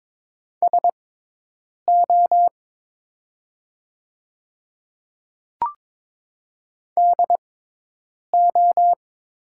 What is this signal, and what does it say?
Morse code, heard as a keyed tone, about 22 wpm, spaced as for 8 wpm, 700 Hz. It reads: SO DO
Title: 6x Character Spacing - Top 100 Words - 22wpm